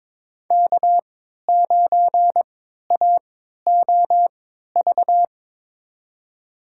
Morse code, heard as a keyed tone, about 22 wpm, spaced as for 13 wpm, 700 Hz. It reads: K9AOV
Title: USA Call Signs - 5 Characters - 3x Character Spacing - 22wpm